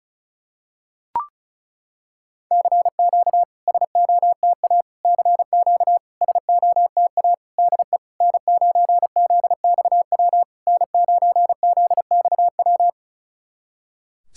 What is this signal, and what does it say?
Morse code, heard as a keyed tone, about 36 wpm, 690 Hz. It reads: CQ SOTA CQ SOTA DE N9ZXW N9ZXW
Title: SOTA QSO Practice 35wpm